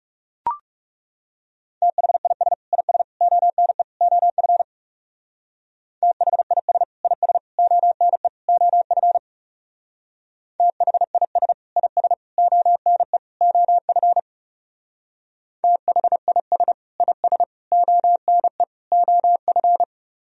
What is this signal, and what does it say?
Morse code, heard as a keyed tone, about 37 wpm, 700 Hz. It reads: THIS IS ONE OF THIS IS ONE OF THIS IS ONE OF THIS IS ONE OF
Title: Speed-Racing - Sets of 4 Words 30wpm